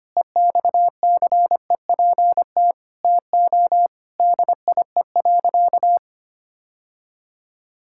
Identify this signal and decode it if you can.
Morse code, heard as a keyed tone, about 25 wpm, 700 Hz. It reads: EXCEPT TO DIE.
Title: Sentences from Top 1600 Words 25wpm